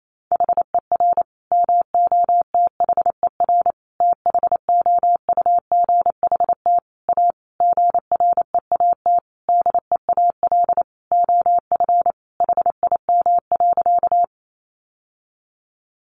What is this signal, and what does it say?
Morse code, heard as a keyed tone, about 28 wpm, 700 Hz. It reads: HER MOTHER THOUGHT A GREAT DEAL OF HIM.
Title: Sentences from Top 800 Words 28wpm